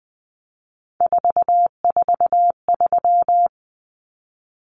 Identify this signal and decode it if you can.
Morse code, heard as a keyed tone, about 20 wpm, 700 Hz. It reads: 443
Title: Common Cut Numbers - 3-Digit 20wpm